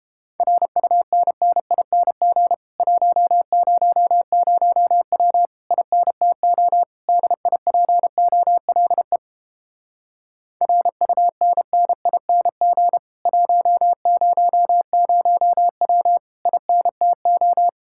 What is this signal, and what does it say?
Morse code, heard as a keyed tone, about 31 wpm, 700 Hz. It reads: RUNNING 100W INTO DIPOLE RUNNING 100W INTO